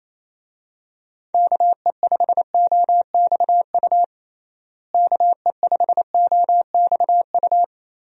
Morse code, heard as a keyed tone, about 28 wpm, 700 Hz. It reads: KE5OXU KE5OXU
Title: USA Call Signs - 6 Characters - Repeated 3x - 28wpm